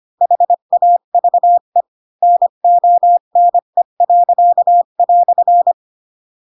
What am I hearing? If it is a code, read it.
Morse code, heard as a keyed tone, about 25 wpm, 700 Hz. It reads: HAVE NONE."
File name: Pinocchio / Pinocchio ch8 20 25 ik1hge